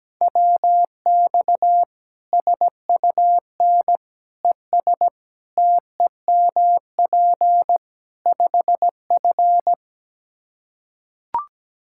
Morse code, heard as a keyed tone, about 17 wpm, 700 Hz. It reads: WX SUN ES TEMP 5F
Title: Practice QSO - Exchange 2 - Info - 17wpm